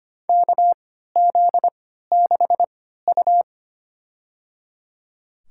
Morse code, heard as a keyed tone, about 25 wpm, spaced as for 14 wpm, 700 Hz. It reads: KZ6U
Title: USA Call Signs - 4 Characters - 3x Character Spacing - 25wpm